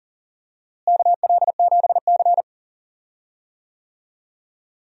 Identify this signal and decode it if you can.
Morse code, heard as a keyed tone, about 38 wpm, 700 Hz. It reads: KL7C
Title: DXCC-100 Call Signs - 4 Characters - Repeated 2x - 40wpm